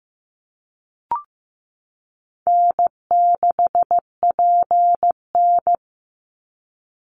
Morse code, heard as a keyed tone, about 15 wpm, 700 Hz. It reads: N6PN
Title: Easy Call Signs Repeated 3x - 15wpm